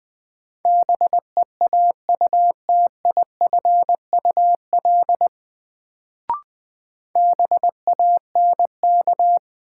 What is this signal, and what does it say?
Morse code, heard as a keyed tone, about 20 wpm, 700 Hz. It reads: BEAUTIFUL BANK
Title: Sentences from Top 900 Words - New Word Drill - 20wpm